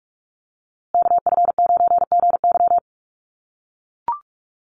Morse code, heard as a keyed tone, about 44 wpm, 700 Hz. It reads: KF9GY
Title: USA Call Signs Repeated 2x 45wpm